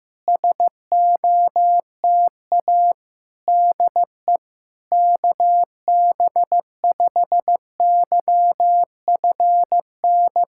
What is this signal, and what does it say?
Morse code, heard as a keyed tone, about 15 wpm, 700 Hz. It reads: SOTA DE KB5YFN
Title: SOTA QSO Practice 15wpm